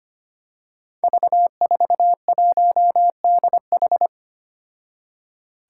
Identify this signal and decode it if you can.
Morse code, heard as a keyed tone, about 25 wpm, 700 Hz. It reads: V41DH
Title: Five Challenging Letter-Number Combinations 25wpm